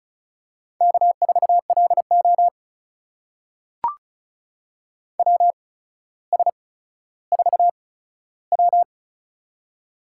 Morse code, heard as a keyed tone, about 36 wpm, spaced as for 15 wpm, 700 Hz. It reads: K4LO WS4W
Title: USA Call Signs - 4 Characters - 8x Character Spacing - 35wpm